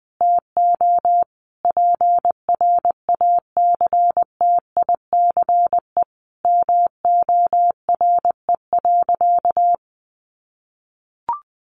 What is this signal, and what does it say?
Morse code, heard as a keyed tone, about 20 wpm, 700 Hz. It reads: TO PRACTICE MORE.
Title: Sentences from Top 700 Words 20wpm